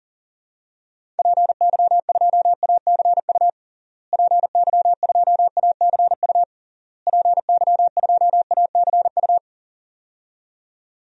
Morse code, heard as a keyed tone, about 37 wpm, 700 Hz. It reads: PY2ACU PY2ACU PY2ACU
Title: DXCC-100 Call Signs - 6 Characters - Repeated 3x - 40wpm